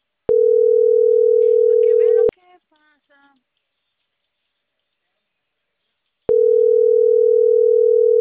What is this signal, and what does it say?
Ringback tone (North America), 440 + 480 Hz.